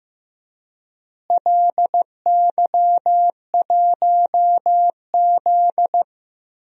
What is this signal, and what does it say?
Morse code, heard as a keyed tone, about 15 wpm, 700 Hz. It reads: LY1Z
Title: DXCC-100 Call Signs - 4 Characters - Repeated 3x - 15wpm